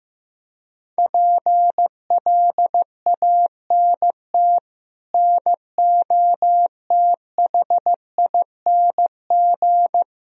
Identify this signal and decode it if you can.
Morse code, heard as a keyed tone, about 15 wpm, 700 Hz. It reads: PLANT NOTHING
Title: Sentences from Top 1000 Words - Review Part 1 - 15wpm